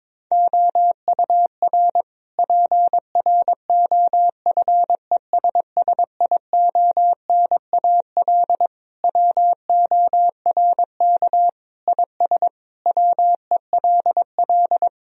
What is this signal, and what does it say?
Morse code, heard as a keyed tone, about 22 wpm, 700 Hz. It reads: OUR PROFESSIONAL WORK IS WELL
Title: Sentences from Top 1000 Words - Review Part 4 - 22wpm